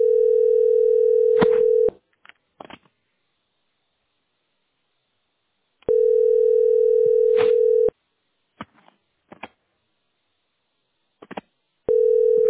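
Ringback tone (North America), 440 + 480 Hz.
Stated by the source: Telephone call recording from the tesspb.mdpbx.fs PBX.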